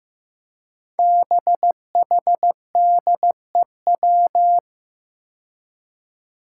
Morse code, heard as a keyed tone, about 15 wpm, 700 Hz. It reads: BHDEW
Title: Five Challenging Letter-Number Combinations 15wpm